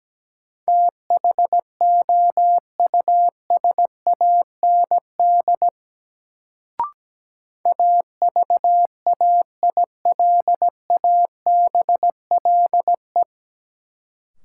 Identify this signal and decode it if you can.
Morse code, heard as a keyed tone, about 17 wpm, 700 Hz. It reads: THOUSAND AVAILABLE
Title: Sentences from Top 600 Words - New Word Drill - 17wpm